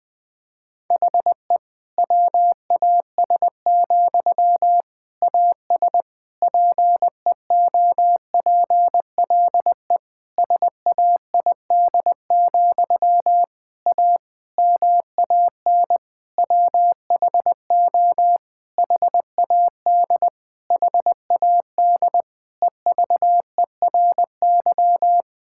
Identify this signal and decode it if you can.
Morse code, heard as a keyed tone, about 20 wpm, 700 Hz. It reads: HE WAS, AS PEOPLE SAID, A MAN WHO HAD HAD EVERY
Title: Sentences from Top 1000 Words - Review Part 6 - 20wpm